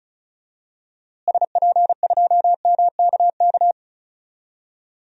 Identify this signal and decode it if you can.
Morse code, heard as a keyed tone, about 36 wpm, 700 Hz. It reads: SP2MKK
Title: DXCC-100 Call Signs - 6 Characters - Repeated 3x - 35wpm